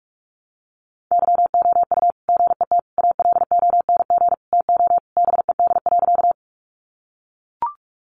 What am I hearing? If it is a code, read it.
Morse code, heard as a keyed tone, about 44 wpm, 700 Hz. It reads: YOU GET ALONG TO BED.
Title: Sentences from Top 700 Words 45wpm